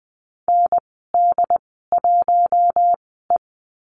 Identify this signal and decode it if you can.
Morse code, heard as a keyed tone, about 20 wpm, spaced as for 14 wpm, 700 Hz. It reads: ND1E
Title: 069 Lesson 12g - Letter d - Call Signs 20wpm